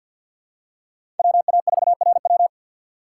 Morse code, heard as a keyed tone, about 52 wpm, 700 Hz. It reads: WA4RW
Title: USA Call Signs - 5 Characters - 3x Character Spacing - 50wpm